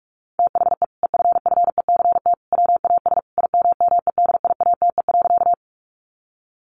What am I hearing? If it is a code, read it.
Morse code, heard as a keyed tone, about 44 wpm, 700 Hz. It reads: THE EFFECT WAS IMMEDIATE.
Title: Sentences from Top 1200 Words 45wpm